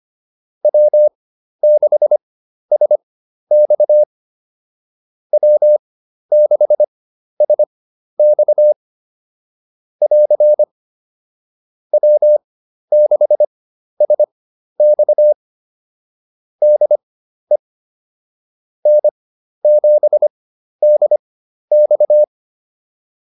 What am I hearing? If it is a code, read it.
Morse code, heard as a keyed tone, about 25 wpm, spaced as for 12 wpm, 600 Hz. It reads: W6SX W6SX <AR> W6SX DE N7DX